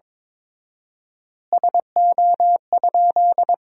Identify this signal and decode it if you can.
Morse code, heard as a keyed tone, about 22 wpm, 700 Hz. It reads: SO?
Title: 4x Word-Spacing - Sentences from Top 400 Words - 22wpm